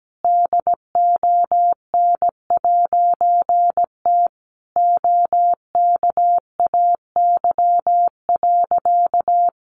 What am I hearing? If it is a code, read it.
Morse code, heard as a keyed tone, about 17 wpm, 700 Hz. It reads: DON'T OKAY.
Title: A Sound Of Thunder 17wpm